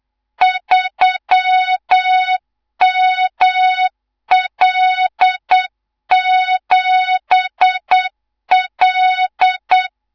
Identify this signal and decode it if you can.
Morse code, heard as a keyed tone, about 8 wpm, 3700 Hz, with noise. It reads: 3ML7L
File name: Morze-HA2MN / hang / 18 IRASJELEK2